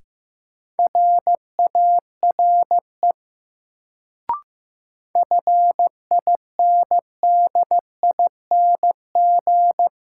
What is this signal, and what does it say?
Morse code, heard as a keyed tone, about 15 wpm, 700 Hz. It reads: RARE FINDING
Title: Sentences from Top 1100 Words - New Word Drill Alt - 15wpm